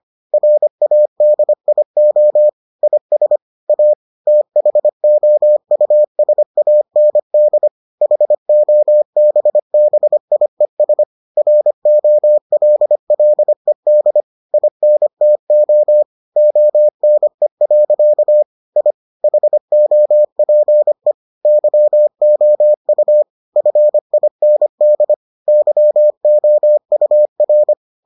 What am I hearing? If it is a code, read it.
Morse code, heard as a keyed tone, about 25 wpm, 600 Hz. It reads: RADIO IS A THOUSAND HOBBIES ROLLED INTO ONE. I HOPE YOU FIND YOUR